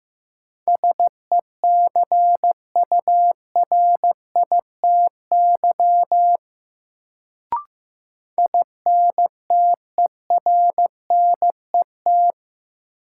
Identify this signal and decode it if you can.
Morse code, heard as a keyed tone, about 15 wpm, 700 Hz. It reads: SECURITY INTERNET
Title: Sentences from Top 1000 Words - Review Part 4 - New Word Drill - 15wpm